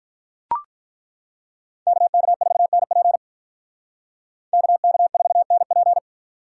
Morse code, heard as a keyed tone, about 42 wpm, 700 Hz. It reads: KK4NP KK4NP
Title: Speed Racing - USA Call Signs 35wpm